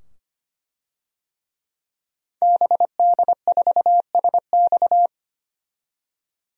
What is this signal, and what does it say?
Morse code, heard as a keyed tone, about 25 wpm, 700 Hz. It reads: BD4SX
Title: DXCC-100 Call Signs - 5 Characters - 25wpm